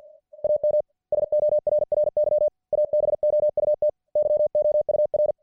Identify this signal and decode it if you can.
Morse code, heard as a keyed tone, about 53 wpm, 600 Hz. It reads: AM SORRY ABOUT YOUR